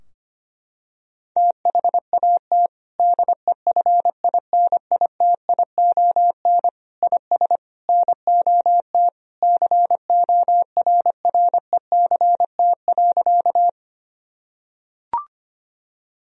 Morse code, heard as a keyed tone, about 25 wpm, 700 Hz. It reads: THAT DEFINITION IS NOT CORRECT.
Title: Sentences from Top 1000 Words - Review Part 4 - 25wpm